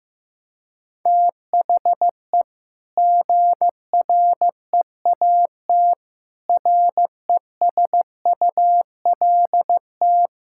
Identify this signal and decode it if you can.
Morse code, heard as a keyed tone, about 16 wpm, 700 Hz. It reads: THE GREAT RESULT
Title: Sentences from Top 1000 Words - Review Part 5 - 15wpm